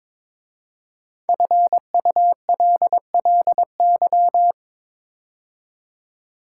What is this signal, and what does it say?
Morse code, heard as a keyed tone, about 22 wpm, 700 Hz. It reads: FULLY
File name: 5 Letter Words - First 3 Letters Given - 22wpm